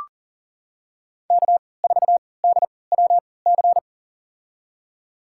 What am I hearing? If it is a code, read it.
Morse code, heard as a keyed tone, about 40 wpm, spaced as for 23 wpm, 700 Hz. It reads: K4DWC
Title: USA Call Signs - 5 Characters - 3x Character Spacing - 40wpm